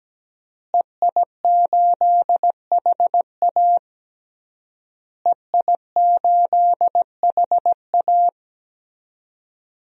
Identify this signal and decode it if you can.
Morse code, heard as a keyed tone, about 17 wpm, 700 Hz. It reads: EI8HA EI8HA
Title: DXCC-100 Call Signs - 5 Characters - Repeated 3x - 17wpm